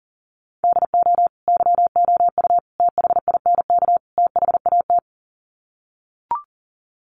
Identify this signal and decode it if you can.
Morse code, heard as a keyed tone, about 40 wpm, 700 Hz. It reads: DO YOU THINK THAT
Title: Sets of 4 Words 40wpm